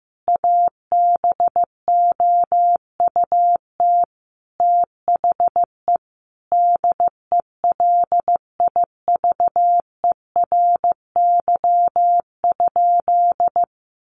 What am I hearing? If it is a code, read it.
Morse code, heard as a keyed tone, about 15 wpm, 700 Hz. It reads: ABOUT THE DELIVERY?
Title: Sentences from Top 1500 Words 15wpm